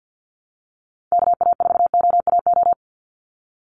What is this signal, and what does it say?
Morse code, heard as a keyed tone, about 50 wpm, 700 Hz. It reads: KA4OAO
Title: USA Call Signs 50wpm